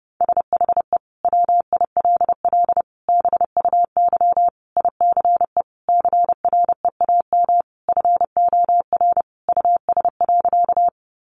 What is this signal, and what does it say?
Morse code, heard as a keyed tone, about 30 wpm, 700 Hz. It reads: SHE WILL BUY ICE CREAM FOR US.
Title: Sentences from Top 1500 Words 30wpm